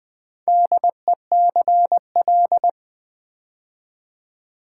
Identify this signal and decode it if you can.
Morse code, heard as a keyed tone, about 20 wpm, 700 Hz. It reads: DECL
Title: Build A Word - 4 Letters At A Time - 11-Letter Words - 20wpm